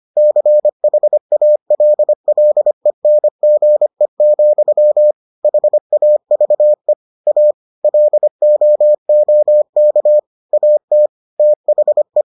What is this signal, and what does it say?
Morse code, heard as a keyed tone, about 25 wpm, 600 Hz. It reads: CHALLENGE, HAVE A LOOK AT THE